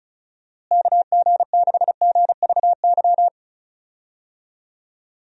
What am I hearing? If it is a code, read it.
Morse code, heard as a keyed tone, about 36 wpm, 700 Hz. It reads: KG6GVY
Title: USA Call Signs - 6 Characters - Repeated 3x - 35wpm